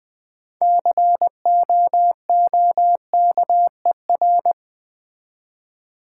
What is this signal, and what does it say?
Morse code, heard as a keyed tone, about 20 wpm, 700 Hz. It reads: COOKER
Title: 6 Letter Words - First 3 Letters Given - 20wpm